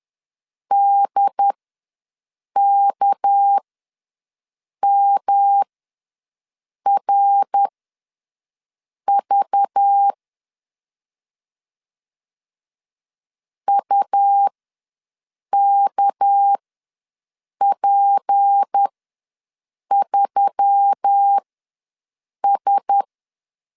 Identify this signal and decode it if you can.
Morse code, heard as a keyed tone, about 10 wpm, spaced as for 6 wpm, 800 Hz. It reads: DKMRV UKP3S
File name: LEZIONI / 6 PKX0 / 6 PK ws 10wpm 3745